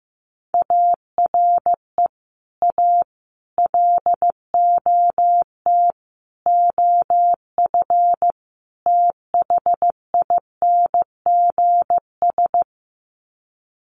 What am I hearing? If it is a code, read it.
Morse code, heard as a keyed tone, about 15 wpm, 700 Hz. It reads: ARE A LOT OF THINGS
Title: Sets of 5 Words Encore2 15wpm